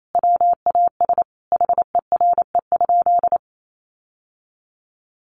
Morse code, heard as a keyed tone, about 28 wpm, 700 Hz. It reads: WAS HERE?
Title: Sentences from Top 1500 Words 28wpm